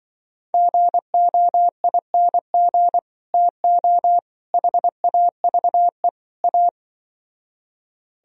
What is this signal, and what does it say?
Morse code, heard as a keyed tone, about 25 wpm, 700 Hz. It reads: GOING TO HAVE A
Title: Speed-Racing - Sets of 4 Words 20wpm